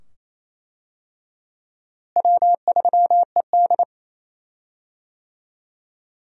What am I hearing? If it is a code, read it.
Morse code, heard as a keyed tone, about 28 wpm, 700 Hz. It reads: W3ED
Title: USA Call Signs - 4 Characters - Repeated 2x - 28wpm